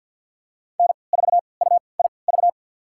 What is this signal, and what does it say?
Morse code, heard as a keyed tone, about 49 wpm, spaced as for 28 wpm, 690 Hz. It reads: N4UIV
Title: USA Call Signs - 5 Characters - 3x Character Spacing - 50wpm